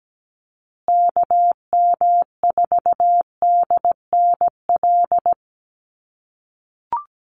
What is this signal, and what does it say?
Morse code, heard as a keyed tone, about 17 wpm, 700 Hz. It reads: KM4DNL
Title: USA Call Signs Repeated 3x 17wpm